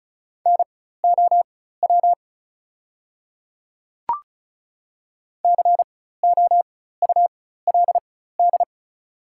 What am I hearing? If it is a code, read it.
Morse code, heard as a keyed tone, about 36 wpm, spaced as for 16 wpm, 700 Hz. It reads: NOW COULD
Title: 4x Character Spacing - Top 100 Words - 35wpm